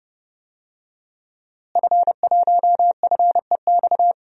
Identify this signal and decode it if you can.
Morse code, heard as a keyed tone, about 32 wpm, 700 Hz. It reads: F1FEX
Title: DXCC-100 Call Signs - 5 Characters - Repeated 2x - 30wpm